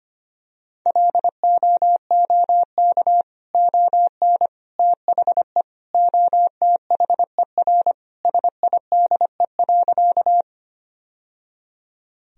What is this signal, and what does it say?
Morse code, heard as a keyed tone, about 25 wpm, 700 Hz. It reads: LOOK ON THE OTHER SIDE.
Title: Sentences from Top 1000 Words - Review Part 1 - 25wpm